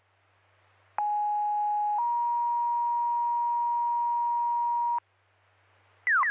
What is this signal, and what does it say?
Two-tone page, heard as a one-second tone, then a three-second tone, about 830 then 950 Hz. It is heard from 1.0 to 5.0 s.